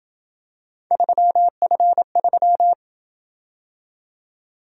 Morse code, heard as a keyed tone, about 27 wpm, 700 Hz. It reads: 3F3